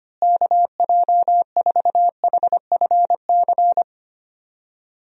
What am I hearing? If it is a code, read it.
Morse code, heard as a keyed tone, about 25 wpm, 700 Hz. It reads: KJ4HFC
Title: USA Call Signs - 6 Characters - Repeated 3x - 25wpm